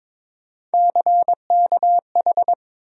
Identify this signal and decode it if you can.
Morse code, heard as a keyed tone, about 22 wpm, 700 Hz. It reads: CKH
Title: USA Call Sign Postfixes - 3 Letters 22wpm